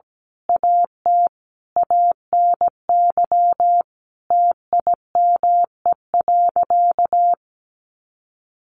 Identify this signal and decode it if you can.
Morse code, heard as a keyed tone, about 17 wpm, 700 Hz. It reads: AT ANY TIME.
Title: Sentences from Top 1500 Words 17wpm